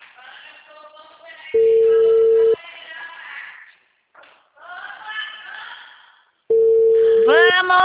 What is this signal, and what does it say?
Ringback tone (China, 450 Hz), with speech.